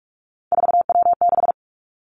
Morse code, heard as a keyed tone, about 46 wpm, 700 Hz. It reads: 4W6
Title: Three Challenging Letter-Number Combinations 45wpm